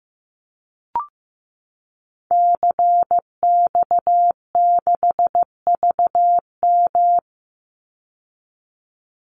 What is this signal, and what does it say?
Morse code, heard as a keyed tone, about 15 wpm, 700 Hz. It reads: CX6VM
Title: Call Signs Repeated 2x - 15wpm